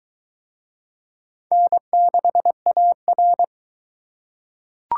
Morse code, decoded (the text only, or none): N6AR